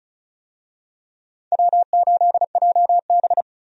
Morse code, heard as a keyed tone, about 34 wpm, 700 Hz. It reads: W8JB
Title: USA Call Signs - 4 Characters - 35wpm